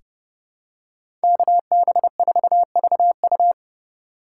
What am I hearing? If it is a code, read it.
Morse code, heard as a keyed tone, about 30 wpm, 700 Hz. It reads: KB4VU
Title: K1USN SST QSO Practice - 30wpm